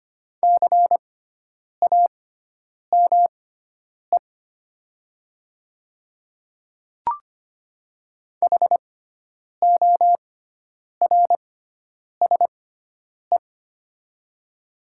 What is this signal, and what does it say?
Morse code, heard as a keyed tone, about 25 wpm, spaced as for 9 wpm, 700 Hz. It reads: CAME HORSE
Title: 6x Character Spacing - Top 500 Words - 25wpm